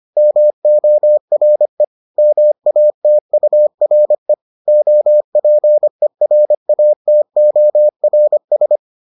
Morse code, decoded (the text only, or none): MORE MATURE OPERATORS